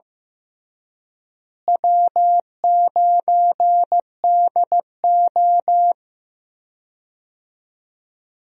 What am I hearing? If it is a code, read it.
Morse code, heard as a keyed tone, about 15 wpm, 700 Hz. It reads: W9DO
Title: USA Call Signs - 4 Characters - Repeated 3x - 15wpm